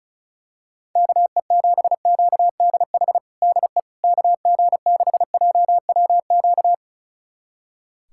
Morse code, heard as a keyed tone, about 36 wpm, 700 Hz. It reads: KE7QDH DE KG6JWQ
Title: USA Callsign DE Callsign - 6 Characters - 35wpm